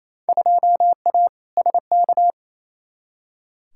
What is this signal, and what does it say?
Morse code, heard as a keyed tone, about 28 wpm, 700 Hz. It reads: 2A SK
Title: Field Day Exchanges Encore - 28wpm